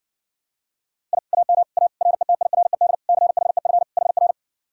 Morse code, heard as a keyed tone, about 60 wpm, 700 Hz. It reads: I AM A RETIRED CHEF HR